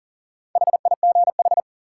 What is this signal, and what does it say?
Morse code, heard as a keyed tone, about 40 wpm, 700 Hz. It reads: HIGH
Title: Sentences from Top 200 Words - New Word Drill - 40wpm